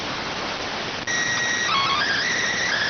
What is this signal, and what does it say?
SSTV (slow-scan television), PD 290.